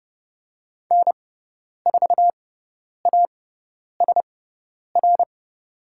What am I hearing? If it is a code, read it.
Morse code, heard as a keyed tone, about 30 wpm, spaced as for 10 wpm, 700 Hz. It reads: N4ASR
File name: Lesson 18g - Letter m - Call Signs 30wpm